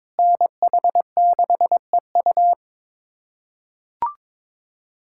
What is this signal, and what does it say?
Morse code, heard as a keyed tone, about 22 wpm, 700 Hz. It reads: NH6EU
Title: USA Callsign DE Callsign - 5 Characters - 22wpm